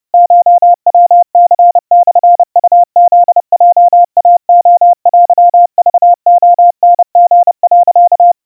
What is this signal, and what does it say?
Morse code (Japanese Wabun code), heard as a keyed tone, about 30 wpm, 700 Hz. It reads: コヤニモウフヲイレテクレタリ、
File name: kawa / Morse sample / kamyu 110 cpm 700 Hz Crispy